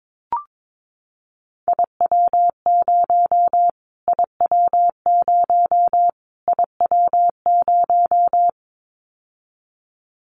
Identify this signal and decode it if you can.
Morse code, heard as a keyed tone, about 22 wpm, 700 Hz. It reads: IW0 IW0 IW0
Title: Callsign Prefixes for Top 25 DXCC - Repeated 3x - 22wpm